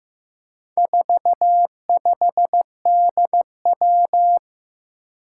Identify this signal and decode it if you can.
Morse code, heard as a keyed tone, about 15 wpm, 700 Hz. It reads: 45DW
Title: Four Challenging Letter-Number Combinations 15wpm